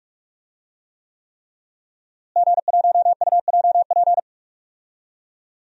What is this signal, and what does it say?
Morse code, heard as a keyed tone, about 44 wpm, 700 Hz. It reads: G1UJP
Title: DXCC-100 Call Signs - 5 Characters - Repeated 3x - 45wpm